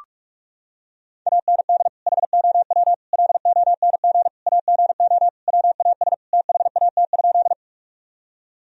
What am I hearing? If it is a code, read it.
Morse code, heard as a keyed tone, about 47 wpm, 700 Hz. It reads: AND HOW LONG AGO WAS THAT?
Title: Sentences from Top 1000 Words - Review Part 3 - 45wpm